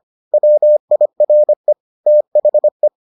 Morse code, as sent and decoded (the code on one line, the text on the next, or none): .-- .. .-. / - .... .
WIRE THE